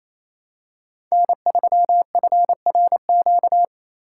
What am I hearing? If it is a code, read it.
Morse code, heard as a keyed tone, about 28 wpm, 700 Hz. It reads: N3FRQ
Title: USA Call Signs - 4x Character Spacing - 28wpm